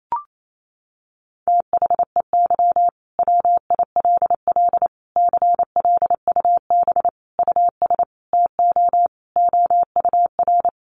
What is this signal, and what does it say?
Morse code, heard as a keyed tone, about 28 wpm, 700 Hz. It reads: THEY WILL CLUB US TO OUR
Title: Sentences from Top 1200 Words 28wpm